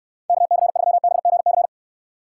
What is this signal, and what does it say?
Morse code, heard as a keyed tone, about 69 wpm, 700 Hz. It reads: KC3BGC